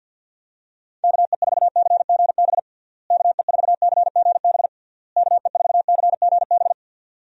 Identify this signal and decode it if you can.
Morse code, heard as a keyed tone, about 49 wpm, 700 Hz. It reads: KE4CGB KE4CGB KE4CGB
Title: USA Call Signs - 6 Characters - Repeated 3x - 50wpm